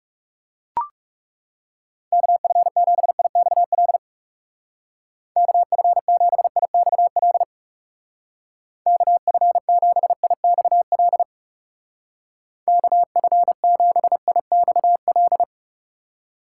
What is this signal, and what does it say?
Morse code, heard as a keyed tone, about 38 wpm, 700 Hz. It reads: KF7IXL KF7IXL KF7IXL KF7IXL